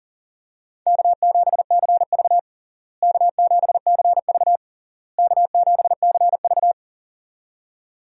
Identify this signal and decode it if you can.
Morse code, heard as a keyed tone, about 40 wpm, 700 Hz. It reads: K7CV K7CV K7CV
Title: USA Call Signs - 4 Characters - Repeated 3x - 40wpm